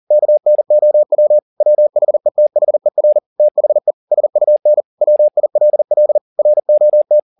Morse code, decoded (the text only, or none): KNOW WHETHER THE SUN WILL ROT